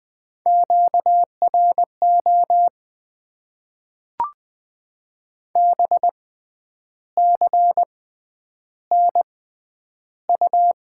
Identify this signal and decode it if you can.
Morse code, heard as a keyed tone, about 20 wpm, spaced as for 9 wpm, 700 Hz. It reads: QRO BCNU
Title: Top 90 QSO Elements - 6x Character Spacing - 20wpm